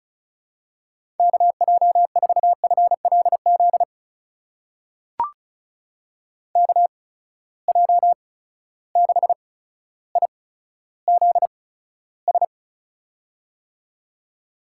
Morse code, heard as a keyed tone, about 36 wpm, spaced as for 15 wpm, 700 Hz. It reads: KJ4FLZ KJ6IZS
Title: USA Call Signs - 6 Characters - 8x Character Spacing - 35wpm